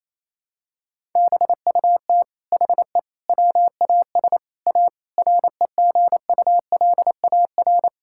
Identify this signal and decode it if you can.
Morse code, heard as a keyed tone, about 28 wpm, 700 Hz. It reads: BUT HE WAS A REGULAR
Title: Sentences from Top 1000 Words - Review Part 6 - 28wpm